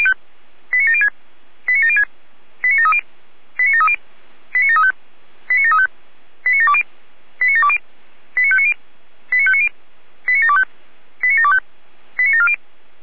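Five-tone selective call (ZVEI), heard as quick runs of five tones, tones of about 70 ms. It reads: 89706 89706 89730 89730 89735 89735 89720 89720 89590 89590 89725 89725 89740
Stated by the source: ZVEI five-tone selective call received by SDR.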